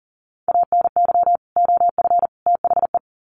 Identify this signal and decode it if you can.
Morse code, heard as a keyed tone, about 40 wpm, 700 Hz. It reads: ANY OF THE
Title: Sets of 3 Words 40wpm